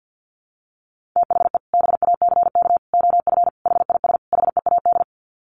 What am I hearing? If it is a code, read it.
Morse code, heard as a keyed tone, about 50 wpm, 700 Hz. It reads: THE BACK OF HIS HEAD
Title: Sets of 5 Words Encore 50wpm